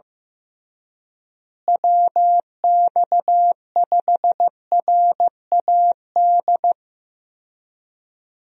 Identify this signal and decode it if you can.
Morse code, heard as a keyed tone, about 15 wpm, 700 Hz. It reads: WX5RAD
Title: USA Call Signs - 6 Characters - Repeated 3x - 15wpm